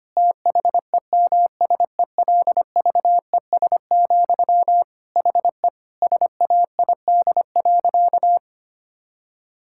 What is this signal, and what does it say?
Morse code, heard as a keyed tone, about 25 wpm, 700 Hz. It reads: THEMSELVES, HE SAID.
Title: Sentences from Top 1000 Words - Review Part 6 - 25wpm